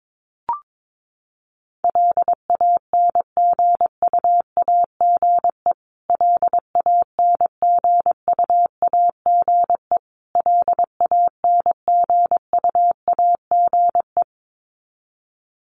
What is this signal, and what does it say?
Morse code, heard as a keyed tone, about 22 wpm, 700 Hz. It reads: LANGUAGE LANGUAGE LANGUAGE
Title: Top 500 Words Repeated 22wpm